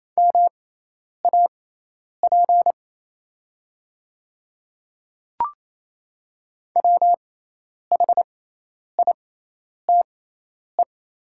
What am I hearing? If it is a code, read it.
Morse code, heard as a keyed tone, about 28 wpm, spaced as for 10 wpm, 700 Hz. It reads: MAP WHITE
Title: 6x Character Spacing - Top 500 Words - 28wpm